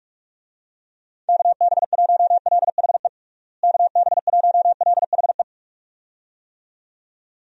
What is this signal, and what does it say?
Morse code, heard as a keyed tone, about 46 wpm, 700 Hz. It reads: KB1LHE KB1LHE
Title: USA Call Signs - 6 Characters - Repeated 2x - 45wpm